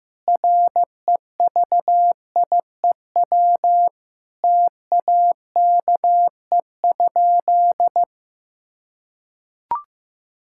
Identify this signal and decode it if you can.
Morse code, heard as a keyed tone, about 16 wpm, 700 Hz. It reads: REVIEW TAKE?
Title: Sentences from Top 1000 Words - Review Part 6 - 15wpm